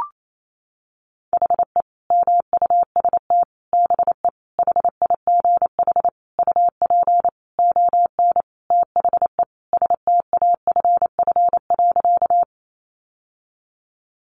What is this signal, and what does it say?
Morse code, heard as a keyed tone, about 28 wpm, 700 Hz. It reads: HE MUST BE HIGH UP ON THE STAFF.
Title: Sentences from Top 1200 Words 28wpm